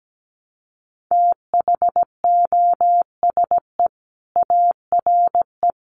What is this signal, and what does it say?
Morse code, heard as a keyed tone, about 17 wpm, 700 Hz. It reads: THOSE ARE
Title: Sentences from Top 1100 Words 17wpm